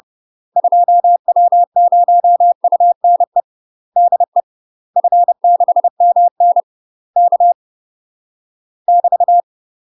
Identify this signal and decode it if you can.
Morse code, heard as a keyed tone, about 30 wpm, 700 Hz. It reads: 2W0UNE DE F6MN K <BT>
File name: UK MASTER QSOs TRAINING / 30WPM / UK MASTER QSOs 0036